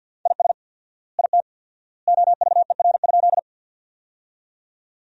Morse code, heard as a keyed tone, about 50 wpm, 700 Hz. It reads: IS IT OVER?
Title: 4x Word-Spacing - Sentences from Top 100 Words - 50wpm